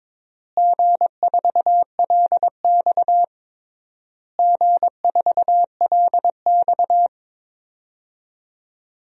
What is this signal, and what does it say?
Morse code, heard as a keyed tone, about 22 wpm, 700 Hz. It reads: G4LX G4LX
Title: DXCC-100 Call Signs - 4 Characters - Repeated 3x - 22wpm